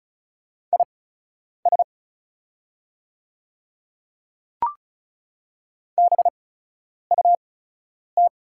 Morse code, heard as a keyed tone, about 36 wpm, spaced as for 10 wpm, 690 Hz. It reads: IS BUT
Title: 8x Character Spacing - Top 100 Words - 35wpm